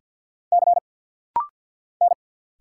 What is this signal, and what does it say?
Morse code, heard as a keyed tone, about 48 wpm, 700 Hz. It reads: C N
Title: Single Letter-Number Alt - Rapid-Fire 50wpm